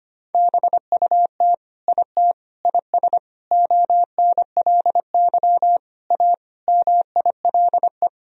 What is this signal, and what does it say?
Morse code, heard as a keyed tone, about 25 wpm, 700 Hz. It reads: BUT IT IS ONLY A MILE
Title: Sentences from Top 1000 Words - Review Part 1 - 25wpm